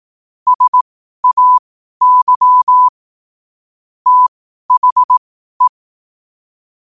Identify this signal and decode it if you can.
Morse code, heard as a keyed tone, about 18 wpm, spaced as for 12 wpm, 1000 Hz. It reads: SAY THE
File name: qotdfiles / 12wpm